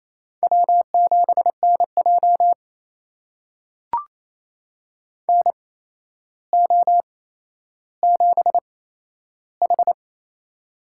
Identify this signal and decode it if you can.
Morse code, heard as a keyed tone, about 28 wpm, spaced as for 12 wpm, 700 Hz. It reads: W7NJ NO7H
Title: USA Call Signs - 4 Characters - 8x Character Spacing - 28wpm